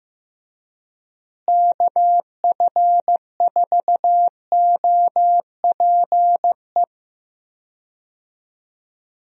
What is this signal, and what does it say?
Morse code, heard as a keyed tone, about 15 wpm, 700 Hz. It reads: KF4OPE
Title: USA Call Signs - 6 Characters - Repeated 2x - 15wpm